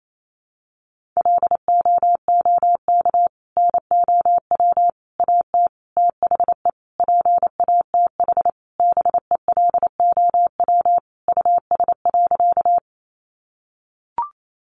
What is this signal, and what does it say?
Morse code, heard as a keyed tone, about 28 wpm, 700 Hz. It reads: LOOK NOW AT THE PATH BELOW US.
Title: Sentences from Top 1400 Words 28wpm